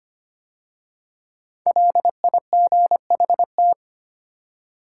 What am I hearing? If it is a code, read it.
Morse code, heard as a keyed tone, about 25 wpm, 700 Hz. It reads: LIGHT
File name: 5 Letter Words - First 3 Letters Given - 25wpm